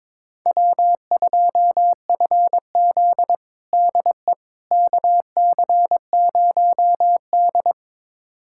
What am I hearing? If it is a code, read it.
Morse code, heard as a keyed tone, about 22 wpm, 700 Hz. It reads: W2FZ DE KC0D
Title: USA Callsign DE Callsign 22wpm